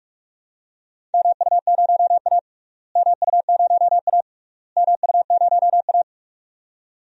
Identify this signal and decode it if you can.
Morse code, heard as a keyed tone, about 46 wpm, 700 Hz. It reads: MU0A MU0A MU0A
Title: DXCC-100 Call Signs - 4 Characters - Repeated 3x - 45wpm